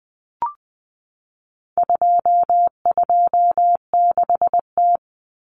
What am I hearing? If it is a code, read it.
Morse code, heard as a keyed tone, about 20 wpm, 700 Hz. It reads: 226T
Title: Common Cut Numbers - 4-Digit 20wpm